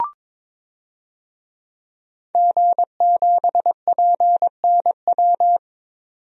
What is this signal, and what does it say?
Morse code, heard as a keyed tone, about 22 wpm, 700 Hz. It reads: G7PNW
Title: DXCC-100 Call Signs - 5 Characters - 22wpm